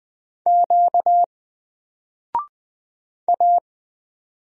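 Morse code, heard as a keyed tone, about 20 wpm, 700 Hz. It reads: Q A
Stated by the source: Single letter and number Morse drill at 20 wpm, from N4LCD.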